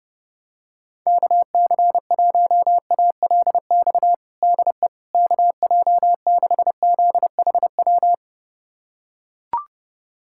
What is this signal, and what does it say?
Morse code, heard as a keyed tone, about 32 wpm, 700 Hz. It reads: KC1ALX DE KJ6ZHW
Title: USA Callsign DE Callsign - 6 Characters - 30wpm